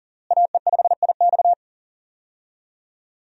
Morse code, heard as a keyed tone, about 40 wpm, 700 Hz. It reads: AE5IX
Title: USA Call Signs - 5 Characters - Repeated 3x - 40wpm